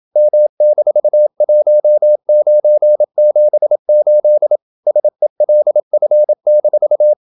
Morse code, heard as a keyed tone, about 27 wpm, 600 Hz. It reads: M-1978 SELF-
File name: Never 0027wpm